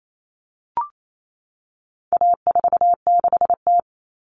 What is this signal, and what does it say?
Morse code, heard as a keyed tone, about 28 wpm, 700 Hz. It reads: A46T
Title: Common Cut Numbers - 4-Digit 28wpm